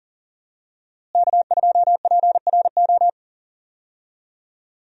Morse code, heard as a keyed tone, about 40 wpm, 700 Hz. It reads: K2PRO
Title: USA Call Signs - 5 Characters - Repeated 3x - 40wpm